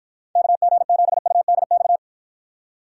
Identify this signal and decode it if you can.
Morse code, heard as a keyed tone, about 52 wpm, 700 Hz. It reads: KG7UDX